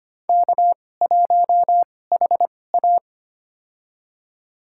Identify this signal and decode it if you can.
Morse code, heard as a keyed tone, about 25 wpm, spaced as for 18 wpm, 700 Hz. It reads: K1HA
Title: USA Call Signs - 4 Characters - 2x Character Spacing - 25wpm